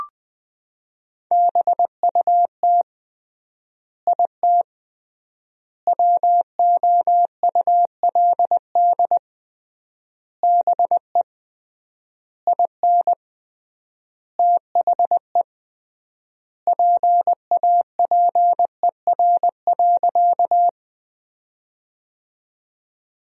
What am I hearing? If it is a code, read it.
Morse code, heard as a keyed tone, about 20 wpm, 700 Hz. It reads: BUT IT WOULD BE IN THE PAPER.
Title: 3x Word-Spacing - Sentences from Top 300 Words - 20wpm